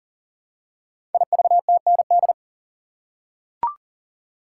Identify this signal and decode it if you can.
Morse code, heard as a keyed tone, about 42 wpm, 690 Hz. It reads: IVTND
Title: Code Groups - 5 Letters - 40wpm